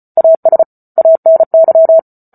Morse code, heard as a keyed tone, about 34 wpm, 650 Hz. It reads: AS ANY